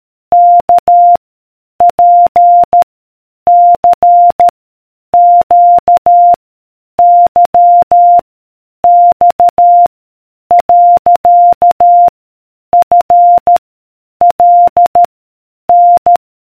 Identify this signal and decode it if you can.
Morse code, heard as a keyed tone, about 13 wpm, 700 Hz. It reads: K P C Q Y X . F L N